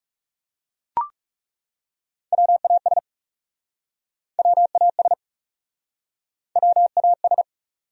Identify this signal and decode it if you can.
Morse code, heard as a keyed tone, about 42 wpm, 700 Hz. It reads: WAS WAS WAS